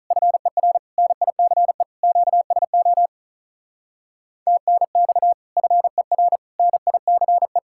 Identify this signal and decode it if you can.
Morse code, heard as a keyed tone, about 47 wpm, 700 Hz. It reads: FER NICE QSO TNX FER NICE